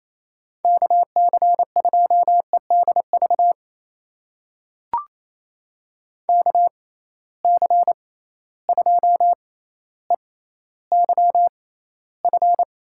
Morse code, heard as a keyed tone, about 28 wpm, spaced as for 14 wpm, 700 Hz. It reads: KC2EDV KC2EYF
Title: USA Call Signs - 6 Characters - 6x Character Spacing - 28wpm